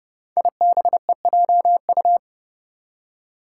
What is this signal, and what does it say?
Morse code, heard as a keyed tone, about 28 wpm, 700 Hz. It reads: IBEJU
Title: Five Challenging Letter-Number Combinations 30wpm